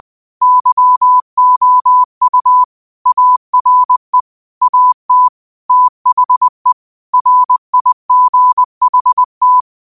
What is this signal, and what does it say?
Morse code, heard as a keyed tone, about 20 wpm, 1000 Hz. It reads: YOU ARE AT THE RIGHT